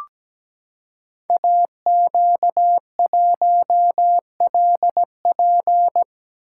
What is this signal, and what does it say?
Morse code, heard as a keyed tone, about 17 wpm, 700 Hz. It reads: AQ1LP
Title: Code Groups - 5 Letter:Numbers - 17wpm